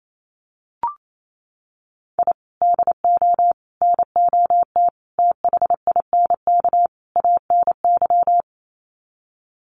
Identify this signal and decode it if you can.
Morse code, heard as a keyed tone, about 28 wpm, 700 Hz. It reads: I DO NOT THINK ANY
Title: Sets of 5 Words Encore 28wpm